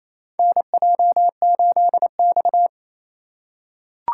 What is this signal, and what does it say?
Morse code, heard as a keyed tone, about 28 wpm, 700 Hz. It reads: NJ8X
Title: USA Call Signs - 4x Character Spacing - 28wpm